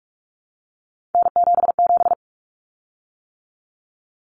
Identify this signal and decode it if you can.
Morse code, heard as a keyed tone, about 44 wpm, 700 Hz. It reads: N77
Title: Common Cut Numbers - 3-Digit 45wpm